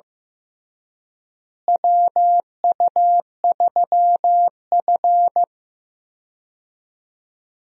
Morse code, heard as a keyed tone, about 16 wpm, 700 Hz. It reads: WU3F
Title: USA Call Signs - 4 Characters - Repeated 3x - 15wpm